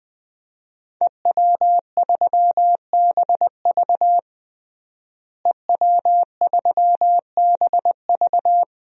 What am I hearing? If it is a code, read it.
Morse code, heard as a keyed tone, about 20 wpm, 700 Hz. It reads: EW3BV EW3BV
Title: DXCC-100 Call Signs - 5 Characters - Repeated 2x - 20wpm